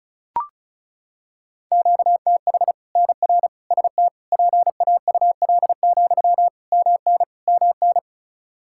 Morse code, heard as a keyed tone, about 36 wpm, 700 Hz. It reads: QTH NR ST PAUL, MN MN
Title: Practice QSO - Exchange 1 - Beginning - 35wpm